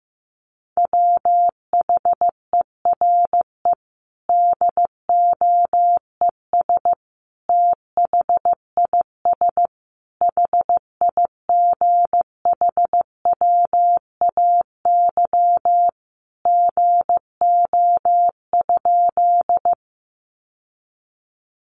Morse code, heard as a keyed tone, about 15 wpm, 700 Hz. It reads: WHERE DOES THIS HIGHWAY GO?
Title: Sentences from Top 1400 Words 15wpm